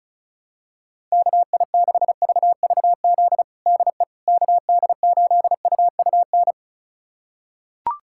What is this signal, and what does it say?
Morse code, heard as a keyed tone, about 34 wpm, 700 Hz. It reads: KI6VVZ DE KD8UUN
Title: USA Callsign DE Callsign - 6 Characters - 35wpm